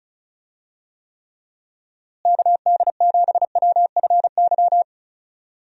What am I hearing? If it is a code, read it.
Morse code, heard as a keyed tone, about 34 wpm, 700 Hz. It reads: KD7WFY
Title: USA Call Signs - 6 Characters - Repeated 2x - 35wpm